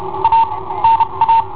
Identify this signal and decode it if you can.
Morse code, heard as a keyed tone, about 32 wpm, spaced as for 26 wpm, 910 Hz. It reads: A NA
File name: i7ohp / cassetta i7ohp B